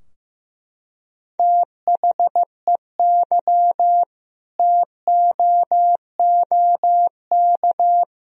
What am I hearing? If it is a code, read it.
Morse code, heard as a keyed tone, about 16 wpm, 700 Hz. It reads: THEY TOOK
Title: Sentences from Top 1000 Words - Review Part 2 - 15wpm